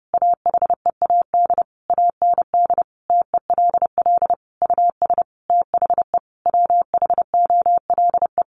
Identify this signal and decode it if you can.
Morse code, heard as a keyed tone, about 30 wpm, 700 Hz. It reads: AHEAD AND TELL US THE WHOLE
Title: Sentences from Top 1100 Words 30wpm